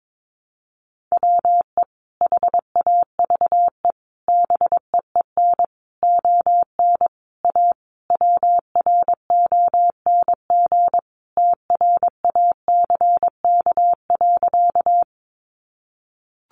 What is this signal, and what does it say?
Morse code, heard as a keyed tone, about 22 wpm, 700 Hz. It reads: WE HAVE BEEN ON A WRONG TRACK.
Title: Sentences from Top 1100 Words 22wpm